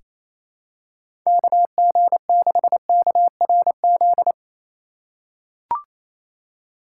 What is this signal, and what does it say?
Morse code, heard as a keyed tone, about 28 wpm, 700 Hz. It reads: KG6KRZ
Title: USA Call Signs - 6 Characters - 2x Character Spacing - 28wpm